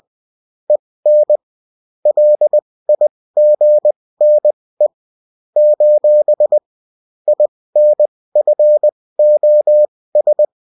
Morse code, heard as a keyed tone, about 20 wpm, spaced as for 16 wpm, 600 Hz. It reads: EN LIGNE : INFOS